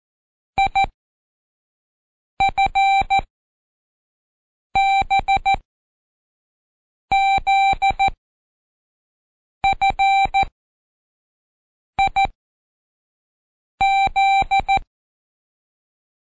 Morse code, heard as a keyed tone, about 14 wpm, spaced as for 5 wpm, 780 Hz. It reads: IFBZFIZ
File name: Code Quick / AudioDownld1 / Seg48-Day6-ReviewAll